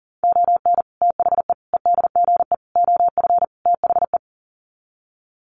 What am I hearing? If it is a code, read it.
Morse code, heard as a keyed tone, about 40 wpm, 700 Hz. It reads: ON THE EDGE OF THE
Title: Sets of 5 Words 40wpm